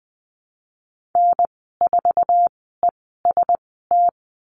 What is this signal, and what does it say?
Morse code, heard as a keyed tone, about 20 wpm, spaced as for 15 wpm, 700 Hz. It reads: N4EST
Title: 048 Lesson 9g - Number 4 - Call Signs 20wpm